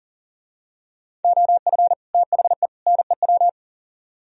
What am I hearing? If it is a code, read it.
Morse code, heard as a keyed tone, about 37 wpm, 700 Hz. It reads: OF THE NEW
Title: Speed-Racing - Sets of 3 Words 40wpm